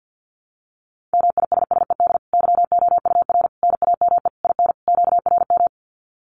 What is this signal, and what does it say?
Morse code, heard as a keyed tone, about 50 wpm, 700 Hz. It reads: MISSED YOUR NAME IN QRM
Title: Common QSO Phrases 50wpm